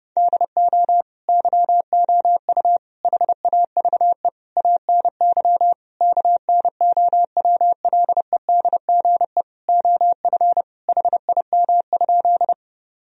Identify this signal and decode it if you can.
Morse code, heard as a keyed tone, about 30 wpm, 700 Hz. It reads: DO YOU HAVE ANY KNOWLEDGE OF HIM?
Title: Sentences from Top 1000 Words - Review Part 3 - 30wpm